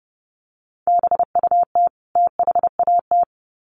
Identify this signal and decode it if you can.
Morse code, heard as a keyed tone, about 30 wpm, 700 Hz. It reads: BUT THAT
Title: Sets of 2 Words 30wpm